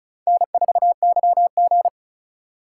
Morse code, heard as a keyed tone, about 34 wpm, 700 Hz. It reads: N4YG
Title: USA Call Signs - 4 Characters - 3x Character Spacing - 35wpm